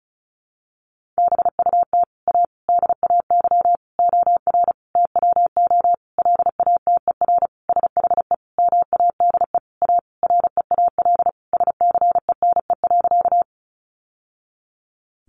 Morse code, heard as a keyed tone, about 34 wpm, 700 Hz. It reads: BUT A DAY OR TWO LATER SHE MADE A REAL SCENE.
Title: Sentences from Top 1100 Words 35wpm